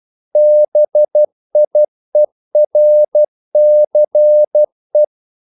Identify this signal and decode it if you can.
Morse code, heard as a keyed tone, about 12 wpm, 600 Hz. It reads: BIERCE
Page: English quotations in CW (telegraphy - morse)